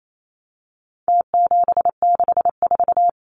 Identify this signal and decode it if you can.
Morse code, heard as a keyed tone, about 28 wpm, 700 Hz. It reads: T764
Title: Numbers - Contest Style - 4-Digit 28wpm